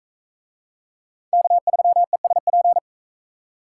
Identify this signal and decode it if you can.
Morse code, heard as a keyed tone, about 38 wpm, 700 Hz. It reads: K3ESP